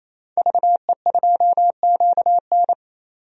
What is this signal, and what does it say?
Morse code, heard as a keyed tone, about 27 wpm, 700 Hz. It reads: VE2QN
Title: DXCC-100 Call Signs - 5 Characters - 28wpm